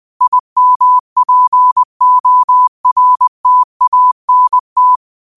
Morse code, heard as a keyed tone, about 20 wpm, 1000 Hz. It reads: IMPORTANT
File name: qotdfiles / 20wpm